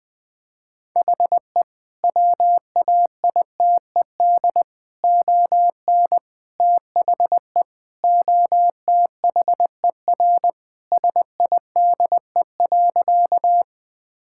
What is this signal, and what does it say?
Morse code, heard as a keyed tone, about 20 wpm, 700 Hz. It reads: HE WAITED ON THE OTHER SIDE.
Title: Sentences with 4-Word Phrases - Spoken Beginning - 20wpm